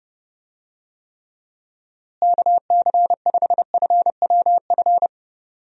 Morse code, heard as a keyed tone, about 32 wpm, 700 Hz. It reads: KC5FWF
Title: USA Call Signs - 6 Characters - Repeated 2x - 30wpm